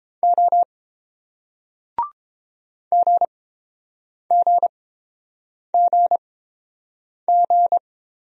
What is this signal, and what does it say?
Morse code, heard as a keyed tone, about 29 wpm, spaced as for 7 wpm, 700 Hz. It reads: O GGGG